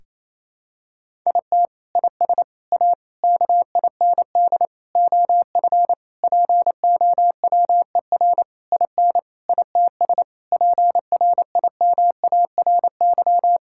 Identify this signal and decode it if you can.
Morse code, heard as a keyed tone, about 28 wpm, 700 Hz. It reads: IT IS A KIND OF POWER IN ITS PRIMARY
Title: Sentences from Top 1000 Words - Review Part 5 - 28wpm